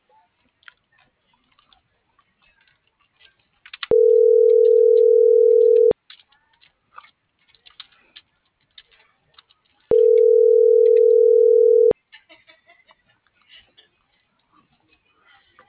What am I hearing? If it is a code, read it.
Ringback tone (North America), 440 + 480 Hz.